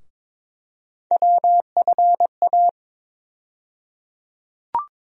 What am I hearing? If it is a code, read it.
Morse code, heard as a keyed tone, about 22 wpm, 700 Hz. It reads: WFA
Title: USA Call Sign Postfixes - 3 Letters 22wpm